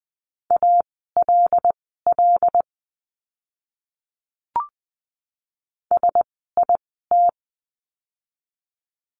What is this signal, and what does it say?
Morse code, heard as a keyed tone, about 20 wpm, spaced as for 15 wpm, 700 Hz. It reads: ALL SIT
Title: 115 Lesson 19f - Letter w - Words 20wpm